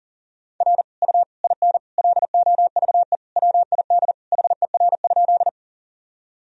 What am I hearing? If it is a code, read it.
Morse code, heard as a keyed tone, about 40 wpm, 700 Hz. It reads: R U IN LOVE WID HER?
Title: Sentences from Top 400 Words and Abbreviations 40wpm